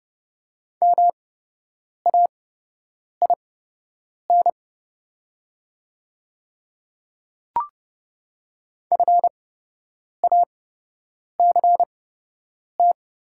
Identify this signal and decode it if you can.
Morse code, heard as a keyed tone, about 30 wpm, spaced as for 8 wpm, 690 Hz. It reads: MAIN FACT
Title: 8x Character Spacing - Top 500 Words - 30wpm